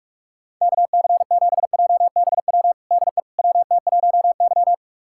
Morse code, heard as a keyed tone, about 46 wpm, 700 Hz. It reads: KC7JBW DE WT1Y
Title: USA Callsign DE Callsign 45wpm